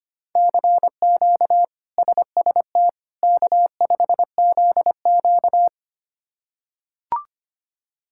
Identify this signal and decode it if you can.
Morse code, heard as a keyed tone, about 25 wpm, 700 Hz. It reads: CQ SST K5ZQ
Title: K1USN SST QSO Practice - 25wpm